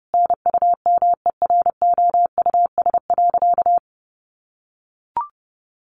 Morse code, heard as a keyed tone, about 30 wpm, 700 Hz. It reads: NUMEROUS.
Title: Sentences from Top 1400 Words 30wpm